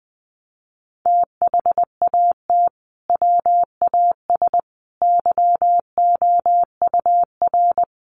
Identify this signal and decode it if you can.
Morse code, heard as a keyed tone, about 20 wpm, 700 Hz. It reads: THAT WAS YOUR
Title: Sentences from Top 600 Words 20wpm